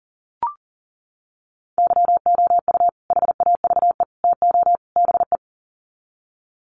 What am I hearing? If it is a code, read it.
Morse code, heard as a keyed tone, about 40 wpm, 700 Hz. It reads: YOU HAVE TO BE
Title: Sets of 4 Words 40wpm